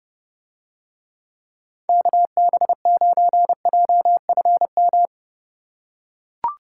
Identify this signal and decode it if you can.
Morse code, heard as a keyed tone, about 30 wpm, 700 Hz. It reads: KB9JFM